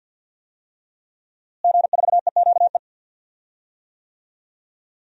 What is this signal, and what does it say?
Morse code, heard as a keyed tone, about 50 wpm, 700 Hz. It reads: G4EQE
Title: DXCC-100 Call Signs - 5 Characters - Repeated 2x - 50wpm